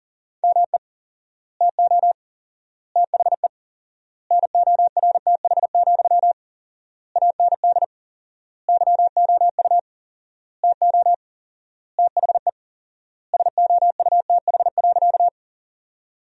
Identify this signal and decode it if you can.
Morse code, heard as a keyed tone, about 40 wpm, 700 Hz. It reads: ME TO THE NORTH, AND YOU TO THE SOUTH.
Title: 4x Word-Spacing - Sentences from Top 400 Words - 40wpm